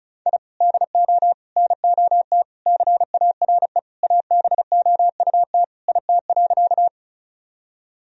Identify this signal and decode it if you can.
Morse code, heard as a keyed tone, about 36 wpm, 700 Hz. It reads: I DO NOT CARE ABOUT IT.
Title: Sentences from Top 1000 Words - Review Part 1 - 35wpm